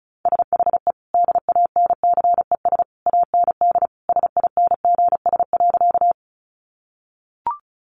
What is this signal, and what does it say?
Morse code, heard as a keyed tone, about 34 wpm, 700 Hz. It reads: SHE DANCES AND SINGS.
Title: Easy Phrases 35wpm